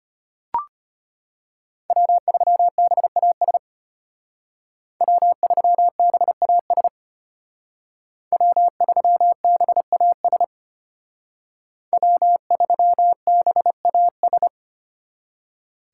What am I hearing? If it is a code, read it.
Morse code, heard as a keyed tone, about 30 wpm, 700 Hz. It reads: W3BAS W3BAS W3BAS W3BAS
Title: Speed Racing - USA 1x3 Call Signs 25wpm